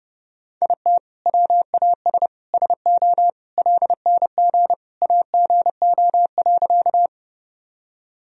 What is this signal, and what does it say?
Morse code, heard as a keyed tone, about 30 wpm, 700 Hz. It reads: IT WAS SO LNG AGO.
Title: Sentences from Top 500 Words and Abbreviations 30wpm